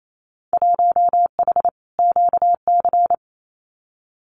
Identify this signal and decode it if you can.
Morse code, heard as a keyed tone, about 28 wpm, 700 Hz. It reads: 1H QC
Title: Winter Field Day Exchanges 28wpm